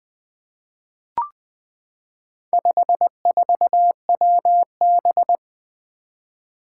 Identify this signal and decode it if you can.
Morse code, heard as a keyed tone, about 20 wpm, 700 Hz. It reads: 54WB
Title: Four Challenging Letter-Number Combinations 20wpm